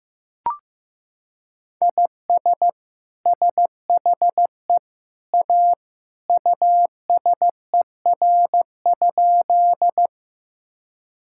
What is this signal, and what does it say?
Morse code, heard as a keyed tone, about 15 wpm, 700 Hz. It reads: IS SHE A USER?
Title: Sentences from Top 1000 Words - Review Part 5 - 15wpm